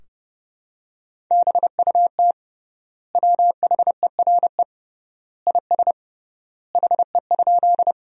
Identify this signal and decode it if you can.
Morse code, heard as a keyed tone, about 30 wpm, 700 Hz. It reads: BUT WHERE IS HE?
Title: 3x Word-Spacing - Sentences from Top 200 Words - 30wpm